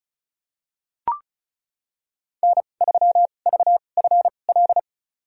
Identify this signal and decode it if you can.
Morse code, heard as a keyed tone, about 36 wpm, spaced as for 25 wpm, 700 Hz. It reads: N3VFL
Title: USA Call Signs - 5 Characters - 2x Character Spacing - 35wpm